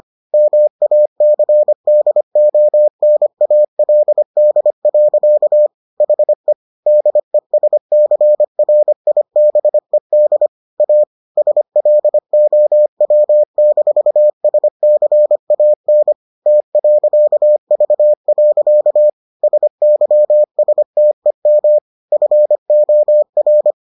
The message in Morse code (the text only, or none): MACDONALD. HE DESCRIBED A SLOW-SCAN T.V. SYSTEM FOR